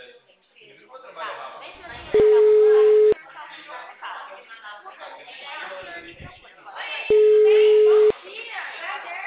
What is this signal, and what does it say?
Ringback tone (Europe, 425 Hz), with speech.